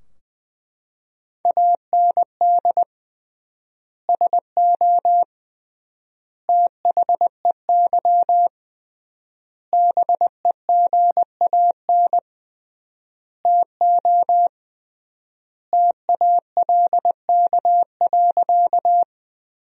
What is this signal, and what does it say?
Morse code, heard as a keyed tone, about 21 wpm, 700 Hz. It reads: AND SO THEY BEGAN TO TALK.
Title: 3x Word-Spacing - Sentences from Top 400 Words - 20wpm